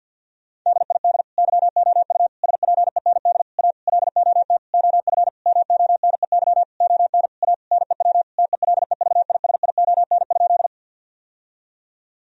Morse code, heard as a keyed tone, about 50 wpm, 700 Hz. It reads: DID YOU SPEND A LOT OF MONEY ON A NEW TELEVISION?
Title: Sentences from Top 1000 Words - Review Part 4 - 50wpm